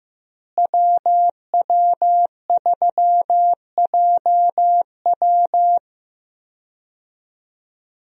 Morse code, heard as a keyed tone, about 15 wpm, 700 Hz. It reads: WW3JW
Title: USA Call Signs - 5 Characters - Repeated 3x - 15wpm